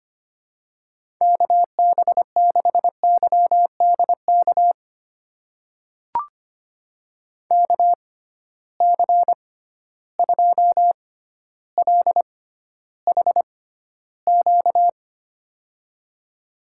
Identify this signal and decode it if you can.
Morse code, heard as a keyed tone, about 25 wpm, spaced as for 13 wpm, 700 Hz. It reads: KB6YDK KC2LHQ
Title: USA Call Signs - 6 Characters - 6x Character Spacing - 25wpm